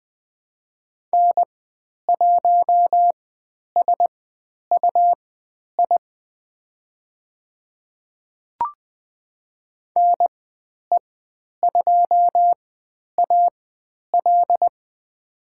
Morse code, heard as a keyed tone, about 20 wpm, spaced as for 10 wpm, 700 Hz. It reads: N1SUI NE2AL
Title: 109 Lesson 18g - Letter m - Call Signs 20wpm